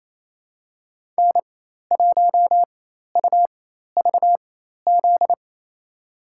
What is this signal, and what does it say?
Morse code, heard as a keyed tone, about 28 wpm, spaced as for 13 wpm, 700 Hz. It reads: N1UVZ
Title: USA Call Signs - 4x Character Spacing - 28wpm